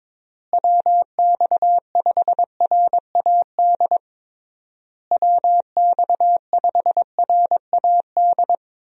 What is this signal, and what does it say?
Morse code, heard as a keyed tone, about 22 wpm, 700 Hz. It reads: WX5RAD WX5RAD
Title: USA Call Signs - 6 Characters - Repeated 3x - 22wpm